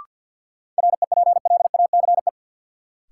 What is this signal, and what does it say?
Morse code, heard as a keyed tone, about 50 wpm, 700 Hz. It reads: REPLACE
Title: Sentences from Top 1400 Words - New Word Drill Alt - 50wpm